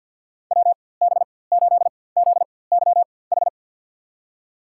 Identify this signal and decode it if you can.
Morse code, heard as a keyed tone, about 46 wpm, spaced as for 23 wpm, 700 Hz. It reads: WB8ZYH
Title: USA Call Signs - 6 Characters - 4x Character Spacing - 50wpm